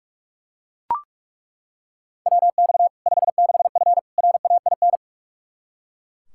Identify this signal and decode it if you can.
Morse code, heard as a keyed tone, about 43 wpm, 700 Hz. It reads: WX 56F RAIN
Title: Practice QSO - Exchange 2 - Info - 45wpm